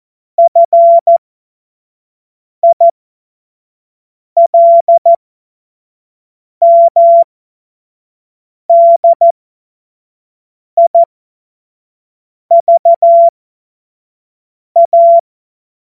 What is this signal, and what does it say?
Morse code, heard as a keyed tone, about 14 wpm, spaced as for 5 wpm, 680 Hz. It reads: FILMDIVA
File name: Morsekurs / Witze / Witz 06. September 2022 25 F70